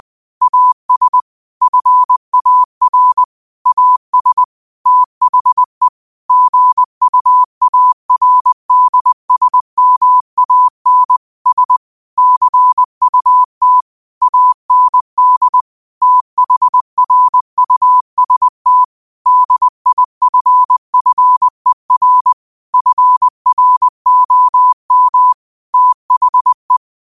Morse code, heard as a keyed tone, about 20 wpm, 1000 Hz. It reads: AS FAR AS THE GUARDSMAN S CUT AND THRUST DIFFER FROM THE
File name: qotdfiles / 20wpm